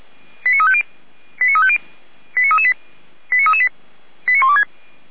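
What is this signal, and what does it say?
Five-tone selective call (ZVEI), heard as quick runs of five tones, tones of about 70 ms. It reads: 89350 89350 89338 89338 89126
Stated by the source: ZVEI five-tone selective call received on an SDR, from the server of Feuerwehr Orlamünde (fire brigade).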